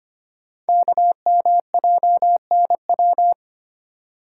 Morse code, heard as a keyed tone, about 25 wpm, 700 Hz. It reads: KMJNW
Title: Code Groups - 5 Letters - 25wpm